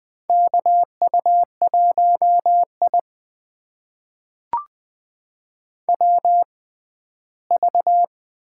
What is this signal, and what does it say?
Morse code, heard as a keyed tone, about 20 wpm, spaced as for 14 wpm, 700 Hz. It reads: KU1I WV
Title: USA Call Signs - 6x Character Spacing - 20wpm